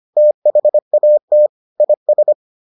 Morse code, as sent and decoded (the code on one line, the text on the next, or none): - .... .- - / .. ...
THAT IS